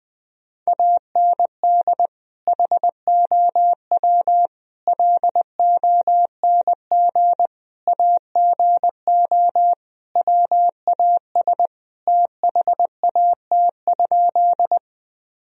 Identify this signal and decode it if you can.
Morse code, heard as a keyed tone, about 20 wpm, 700 Hz. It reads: AND HOW LONG AGO WAS THAT?
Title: Sentences from Top 1000 Words - Review Part 3 - 20wpm